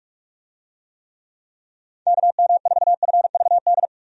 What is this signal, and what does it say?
Morse code, heard as a keyed tone, about 46 wpm, 700 Hz. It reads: KM4FVD
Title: USA Call Signs - 6 Characters - 45wpm